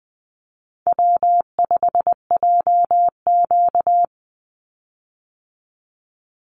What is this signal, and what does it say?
Morse code, heard as a keyed tone, about 20 wpm, 700 Hz. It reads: W5JQ
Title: Call Signs Repeated 2x - 20wpm